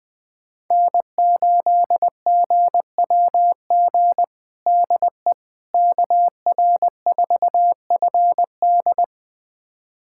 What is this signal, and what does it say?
Morse code, heard as a keyed tone, about 20 wpm, 700 Hz. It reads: N8GWG DE KR4FD
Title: USA Callsign DE Callsign - 5 Characters - 20wpm